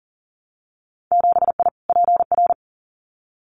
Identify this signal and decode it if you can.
Morse code, heard as a keyed tone, about 40 wpm, 700 Hz. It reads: 7I PR
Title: Winter Field Day Exchanges Encore 40wpm